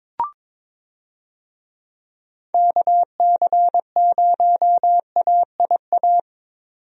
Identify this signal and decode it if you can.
Morse code, heard as a keyed tone, about 22 wpm, 700 Hz. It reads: KC0AIA
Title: USA Call Signs - 6 Characters - 22wpm